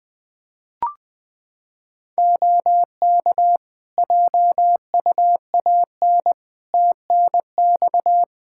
Morse code, heard as a keyed tone, about 20 wpm, 700 Hz. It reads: OK JUAN TNX
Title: Practice QSO - Exchange 4 - Ending - 20wpm